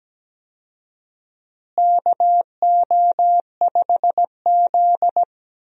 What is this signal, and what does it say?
Morse code, heard as a keyed tone, about 17 wpm, 700 Hz. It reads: KO5Z
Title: USA Call Signs - 4 Characters - Repeated 3x - 17wpm